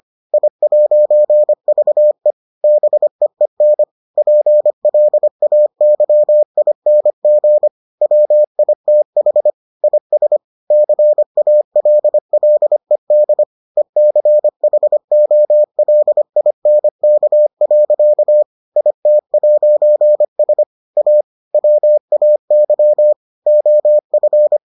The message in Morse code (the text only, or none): I'VE BEEN PLAYING WITH IS CALLED ECHOLINK. IT'S A WAY OF